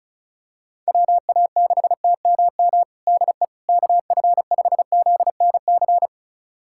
Morse code, heard as a keyed tone, about 34 wpm, 700 Hz. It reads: WA6TMM DE KF5ZNC
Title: USA Callsign DE Callsign - 6 Characters - 35wpm